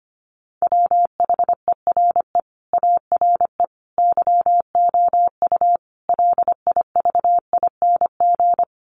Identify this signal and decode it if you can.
Morse code, heard as a keyed tone, about 25 wpm, 690 Hz. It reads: WHERE ARE YOU LIVING
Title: Sentences from Top 1100 Words 25wpm